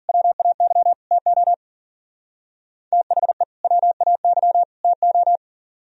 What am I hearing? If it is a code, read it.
Morse code, heard as a keyed tone, about 52 wpm, 700 Hz. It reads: WAY TO THE WAY TO